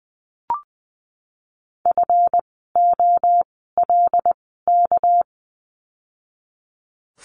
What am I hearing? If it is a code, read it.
Morse code, heard as a keyed tone, about 20 wpm, spaced as for 14 wpm, 700 Hz. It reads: FOLK
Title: 199 Lesson 32f - Letter k - Words 20wpm